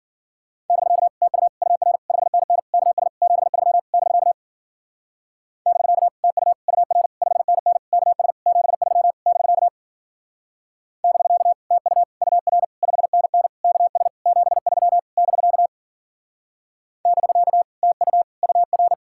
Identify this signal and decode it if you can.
Morse code, heard as a keyed tone, about 49 wpm, 700 Hz. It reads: <BK> TU UR 5NN KS 73 <BK> <BK> TU UR 5NN KS 73 <BK> <BK> TU UR 5NN KS 73 <BK> <BK> TU UR
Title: Speed-Racing Common QSO Phrases - 40wpm